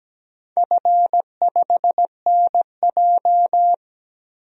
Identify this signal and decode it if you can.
Morse code, heard as a keyed tone, about 17 wpm, 700 Hz. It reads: F5NJ
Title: DXCC-100 Call Signs - 4 Characters - 17wpm